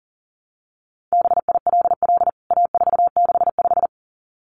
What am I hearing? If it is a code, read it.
Morse code, heard as a keyed tone, about 40 wpm, 700 Hz. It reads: BILL A465
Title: CWT Exchange 40wpm